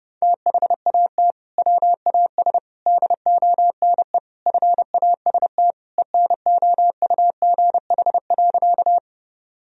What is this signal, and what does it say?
Morse code, heard as a keyed tone, about 30 wpm, 700 Hz. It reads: THAT WAS DONE FAST ENOUGH.
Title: Sentences from Top 1000 Words - Review Part 2 - 30wpm